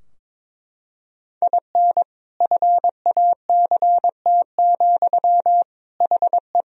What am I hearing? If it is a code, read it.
Morse code, heard as a keyed tone, about 22 wpm, 700 Hz. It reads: IN FACT, HE
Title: Sentences from Top 1000 Words - Review Part 3 - 22wpm